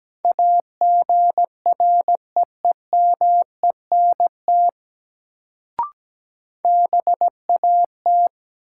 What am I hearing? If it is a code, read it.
Morse code, heard as a keyed tone, about 17 wpm, 700 Hz. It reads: AGREEMENT BAT
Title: Sentences from Top 1500 Words - New Word Drill - 17wpm